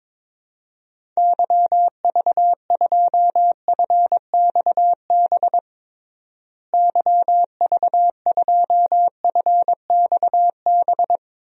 Morse code, heard as a keyed tone, about 21 wpm, 700 Hz. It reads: YV2FXB YV2FXB
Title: DXCC-100 Call Signs - 6 Characters - Repeated 3x - 22wpm